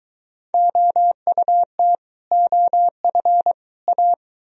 Morse code, heard as a keyed tone, about 22 wpm, 700 Hz. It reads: OUT OF A